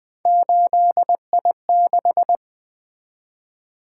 Morse code, heard as a keyed tone, about 20 wpm, 700 Hz. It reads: 8I6
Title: Three Letter-Number Combinations - 20wpm